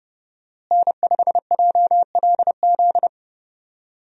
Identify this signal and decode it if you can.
Morse code, heard as a keyed tone, about 29 wpm, 700 Hz. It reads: N5JLZ
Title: USA Call Signs - 5 Characters - Repeated 2x - 30wpm